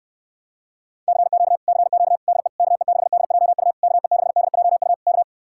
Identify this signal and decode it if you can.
Morse code, heard as a keyed tone, about 69 wpm, 700 Hz. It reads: CQ CQ DE KE6DJU KE6DJU K